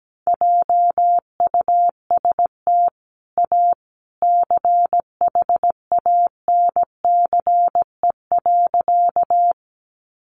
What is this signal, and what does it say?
Morse code, heard as a keyed tone, about 17 wpm, 700 Hz. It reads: JUST A CHANCE.
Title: Sentences from Top 800 Words 17wpm